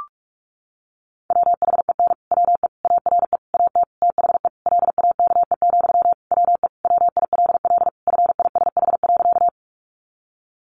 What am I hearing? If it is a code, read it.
Morse code, heard as a keyed tone, about 44 wpm, 700 Hz. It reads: WHEN WE ARE AT THE LAKE, WE WILL FISH.
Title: Sentences from Top 1100 Words 45wpm